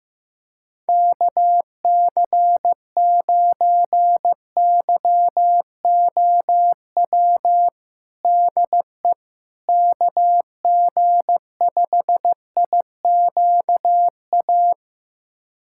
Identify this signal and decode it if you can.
Morse code, heard as a keyed tone, about 15 wpm, 700 Hz. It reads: KC9YOW DE KG5IQA
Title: USA Callsign DE Callsign - 6 Characters - 15wpm